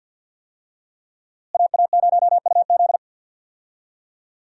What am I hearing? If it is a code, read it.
Morse code, heard as a keyed tone, about 52 wpm, 700 Hz. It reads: AA0UZ
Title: USA Call Signs - 5 Characters - Repeated 2x - 50wpm